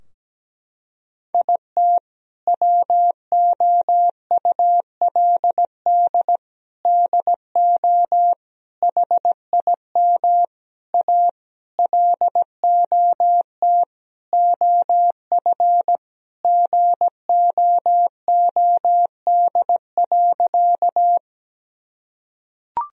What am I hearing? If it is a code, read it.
Morse code, heard as a keyed tone, about 17 wpm, 700 Hz. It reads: IT WOULD DO HIM A LOT OF GOOD.
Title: Sentences from Top 1000 Words - Review Part 3 - 17wpm